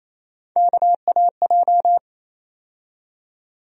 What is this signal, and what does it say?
Morse code, heard as a keyed tone, about 28 wpm, 700 Hz. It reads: KAJ
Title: USA Call Sign Postfixes - 3 Letters 28wpm